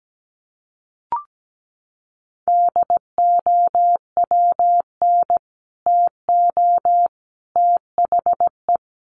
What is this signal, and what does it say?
Morse code, heard as a keyed tone, about 17 wpm, 700 Hz. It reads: DOWN TO THE
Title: Sets of 3 Words 17wpm